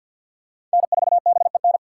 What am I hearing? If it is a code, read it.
Morse code, heard as a keyed tone, about 47 wpm, 700 Hz. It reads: N4BEN
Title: USA Call Signs - 5 Characters - 3x Character Spacing - 50wpm